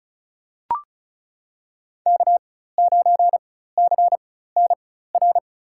Morse code, heard as a keyed tone, about 34 wpm, spaced as for 16 wpm, 700 Hz. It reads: K9CNR
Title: USA Call Signs - 5 Characters - 4x Character Spacing - 35wpm